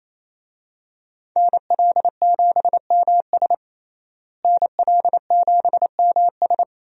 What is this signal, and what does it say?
Morse code, heard as a keyed tone, about 28 wpm, 700 Hz. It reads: NL7MS NL7MS
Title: USA Call Signs - 5 Characters - Repeated 3x - 28wpm